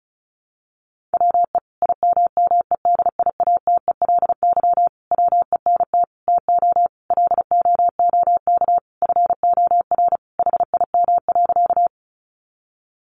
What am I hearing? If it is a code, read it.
Morse code, heard as a keyed tone, about 34 wpm, 700 Hz. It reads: WE IMMEDIATELY WENT TO LOOK FOR HIM.
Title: Sentences from Top 1100 Words 35wpm